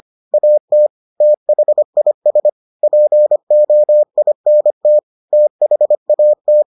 Morse code, heard as a keyed tone, about 25 wpm, 600 Hz. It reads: AT THIS POINT THAT